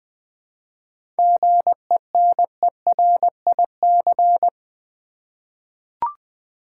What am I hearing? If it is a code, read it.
Morse code, heard as a keyed tone, about 20 wpm, 700 Hz. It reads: GENERIC
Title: 7 Letter Words - First 3 Letters Given - 20wpm